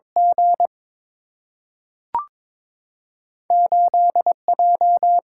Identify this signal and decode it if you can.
Morse code, heard as a keyed tone, about 21 wpm, 700 Hz. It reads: G 8 J
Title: Single-Letter Single-Number 22wpm